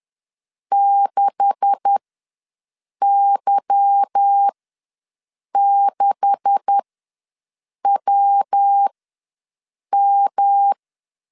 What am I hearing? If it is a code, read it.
Morse code, heard as a keyed tone, about 10 wpm, spaced as for 6 wpm, 800 Hz. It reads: 6Y6WM
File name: LEZIONI / 9 286 ws 10wpm 4032